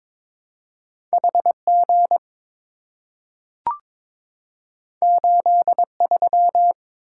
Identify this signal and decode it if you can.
Morse code, heard as a keyed tone, about 23 wpm, 700 Hz. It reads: HG 83
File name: practice / 22wpm / Two Letter-Number Combinations - 22wpm